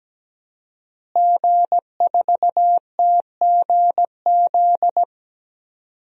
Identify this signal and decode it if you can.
Morse code, heard as a keyed tone, about 17 wpm, 700 Hz. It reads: G4TGZ
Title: DXCC-100 Call Signs - 5 Characters - Repeated 2x - 17wpm